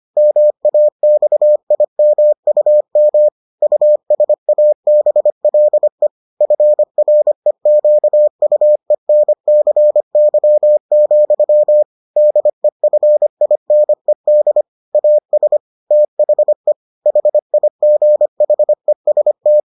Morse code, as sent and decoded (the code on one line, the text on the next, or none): -- .- -..- .. -- ..- -- / ..- ... .- -... .-.. / ..-. .-. --.- ..- . -. -.-. -.-- --..-- / -.. ..-. .. -. -.. / .- ... / - .... / .... .. --. .... ... -
MAXIMUM USABLE FREQUENCY, DEFINED AS THE HIGHEST